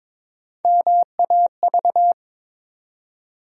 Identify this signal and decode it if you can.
Morse code, heard as a keyed tone, about 22 wpm, 700 Hz. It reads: MAV
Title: USA Call Sign Postfixes - 3 Letters 22wpm